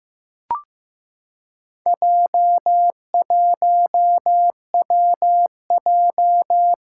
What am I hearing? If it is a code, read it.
Morse code, heard as a keyed tone, about 15 wpm, 700 Hz. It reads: J1WJ
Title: Four Challenging Letter-Number Combinations 15wpm